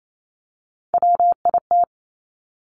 Morse code, heard as a keyed tone, about 28 wpm, 700 Hz. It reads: WIT
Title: Top Three Letter Combinations in Words 28wpm